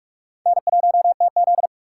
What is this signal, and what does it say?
Morse code, heard as a keyed tone, about 57 wpm, 700 Hz. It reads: N1TZ